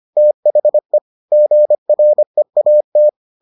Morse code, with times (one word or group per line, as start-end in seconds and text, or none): 0.2-1.0 THE
1.3-3.1 GREAT